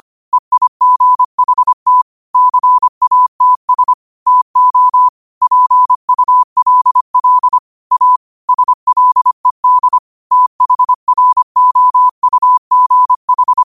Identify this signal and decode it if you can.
Morse code, heard as a keyed tone, about 25 wpm, 1000 Hz. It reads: EIGHT CATS TO PULL A SLED THROUGH
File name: qotdfiles / 25wpm